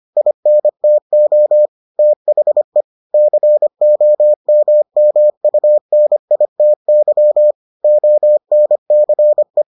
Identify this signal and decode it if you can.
Morse code, heard as a keyed tone, about 25 wpm, 600 Hz. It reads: INTO THE COMMUNITY ONCE